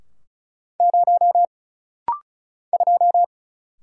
Morse code, heard as a keyed tone, about 35 wpm, 700 Hz. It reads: 0 2
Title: Single Letter-Number Alt - Rapid-Fire 35wpm